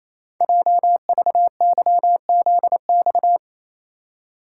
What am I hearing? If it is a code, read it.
Morse code, heard as a keyed tone, about 29 wpm, 700 Hz. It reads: JVYZX
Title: Code Groups - 5 Letters - 28wpm